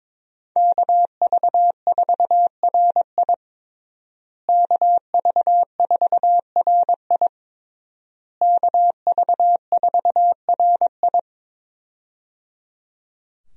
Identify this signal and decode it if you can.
Morse code, heard as a keyed tone, about 22 wpm, 700 Hz. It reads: KV4RI KV4RI KV4RI
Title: USA Call Signs - 5 Characters - Repeated 3x - 22wpm